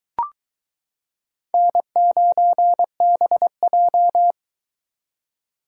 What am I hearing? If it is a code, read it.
Morse code, heard as a keyed tone, about 22 wpm, 700 Hz. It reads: N9BJ